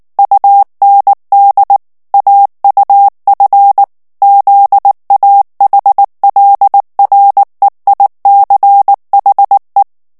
Morse code, heard as a keyed tone, about 19 wpm, 800 Hz. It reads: UND AUF ZAHLREICHE
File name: morse mp3 / ALLEMAND texte clair 2 vitesse 18 mots minute